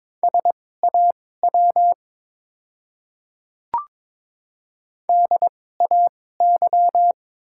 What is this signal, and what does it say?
Morse code, heard as a keyed tone, about 22 wpm, spaced as for 16 wpm, 700 Hz. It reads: SAW DAY
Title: 2x Character Spacing - Top 500 Words - 22wpm